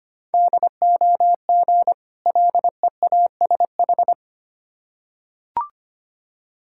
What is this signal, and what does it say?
Morse code, heard as a keyed tone, about 25 wpm, 700 Hz. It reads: DOG LEASH
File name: practice / 25wpm / Difficult Words with Hints - 25wpm